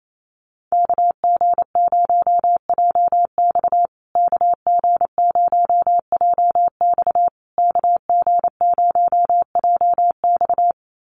Morse code, heard as a keyed tone, about 28 wpm, 700 Hz. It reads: KG0JX KG0JX KG0JX
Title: USA Call Signs Repeated 3x 28wpm